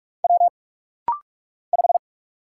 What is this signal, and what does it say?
Morse code, heard as a keyed tone, about 46 wpm, 700 Hz. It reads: W 5
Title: Single Letter-Number Alt - Rapid-Fire 45wpm